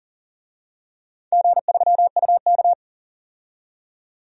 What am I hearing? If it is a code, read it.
Morse code, heard as a keyed tone, about 40 wpm, 700 Hz. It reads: G3UK
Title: DXCC-100 Call Signs - 4 Characters - 40wpm